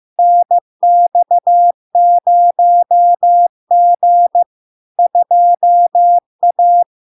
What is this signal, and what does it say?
Morse code, heard as a keyed tone, about 15 wpm, 700 Hz. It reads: NX0G 2A
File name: LARK-FD-2022-Log-in-CW-15wpm-700-Hz